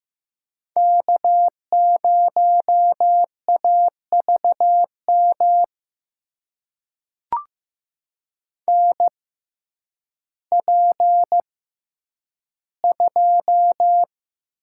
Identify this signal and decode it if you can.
Morse code, heard as a keyed tone, about 15 wpm, spaced as for 9 wpm, 700 Hz. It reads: K0AVM NP2
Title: USA Call Signs - 5 Characters - 6x Character Spacing - 15wpm